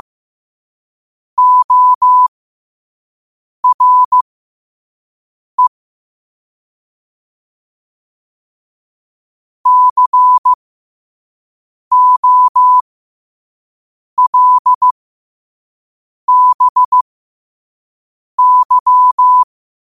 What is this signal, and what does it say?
Morse code, heard as a keyed tone, about 15 wpm, spaced as for 5 wpm, 1000 Hz. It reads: ORE COLBY
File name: qotdfiles / 5wpm